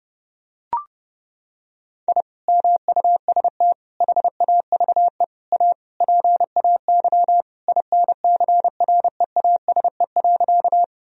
Morse code, heard as a keyed tone, about 32 wpm, 700 Hz. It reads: I MUST HAVE A PAY INCREASE.
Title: Sentences from Top 1000 Words - Review Part 4 - 30wpm